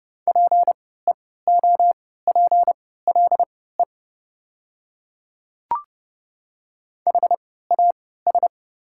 Morse code, heard as a keyed tone, about 30 wpm, spaced as for 17 wpm, 700 Hz. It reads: PEOPLE HAS
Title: 3x Character Spacing - Top 100 Words - 30wpm